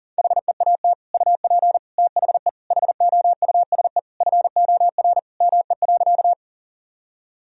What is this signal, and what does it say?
Morse code, heard as a keyed tone, about 37 wpm, 700 Hz. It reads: HEAT UP THE HOUSE FOR ME.
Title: Sentences from Top 1000 Words - Review Part 3 - 40wpm